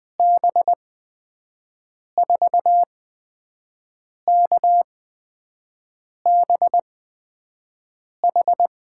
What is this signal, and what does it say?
Morse code, heard as a keyed tone, about 20 wpm, spaced as for 5 wpm, 700 Hz. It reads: B4KBH
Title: USA Call Signs - 6 Characters - 8x Character Spacing - 20wpm